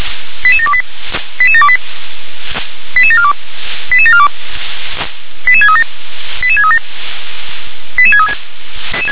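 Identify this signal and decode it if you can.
Five-tone selective call (ZVEI), heard as quick runs of five tones, tones of about 70 ms. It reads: 88728 88728 88742 88742 88637 88637 88636
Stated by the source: ZVEI five-tone selective call received on an SDR.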